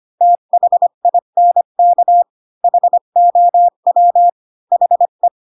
Morse code, heard as a keyed tone, about 25 wpm, 700 Hz. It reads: THINK HOW HE
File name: Pinocchio / Pinocchio ch5 20 25 ik1hge